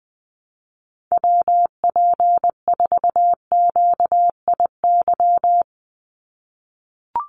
Morse code, heard as a keyed tone, about 20 wpm, 700 Hz. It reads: WP4QIY
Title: USA Call Signs 20wpm